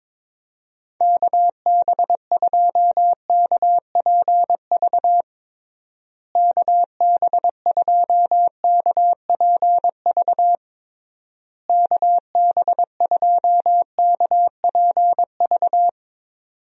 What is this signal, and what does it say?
Morse code, heard as a keyed tone, about 21 wpm, 700 Hz. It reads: KB2KPV KB2KPV KB2KPV
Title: USA Call Signs - 6 Characters - Repeated 3x - 22wpm